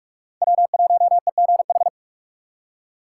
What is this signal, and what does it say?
Morse code, heard as a keyed tone, about 44 wpm, 690 Hz. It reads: W1EGH
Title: USA Call Signs - 5 Characters - 8x Character Spacing - 45wpm